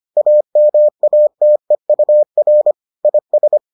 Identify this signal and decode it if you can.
Morse code, heard as a keyed tone, about 25 wpm, 600 Hz. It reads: AMATEUR IS